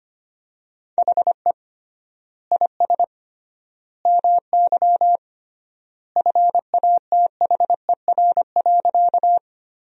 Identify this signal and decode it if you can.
Morse code, heard as a keyed tone, about 25 wpm, 700 Hz. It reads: HE IS MY FATHER.
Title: 3x Word-Spacing - Sentences from Top 300 Words - 25wpm